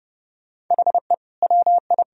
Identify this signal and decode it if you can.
Morse code, heard as a keyed tone, about 32 wpm, 700 Hz. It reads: HE WI
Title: Sentences from Top 1000 Words - Review Part 1 - 30wpm